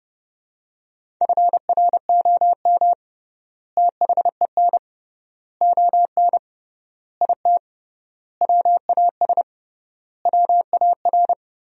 Morse code, heard as a keyed tone, about 30 wpm, 700 Hz. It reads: FROM THEN ON IT WAS WAR
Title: 3x Word-Spacing - Sentences from Top 400 Words - 30wpm